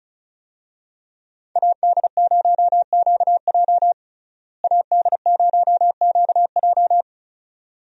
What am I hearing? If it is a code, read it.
Morse code, heard as a keyed tone, about 34 wpm, 700 Hz. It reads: AD0QJ AD0QJ
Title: USA Call Signs - 5 Characters - Repeated 2x - 35wpm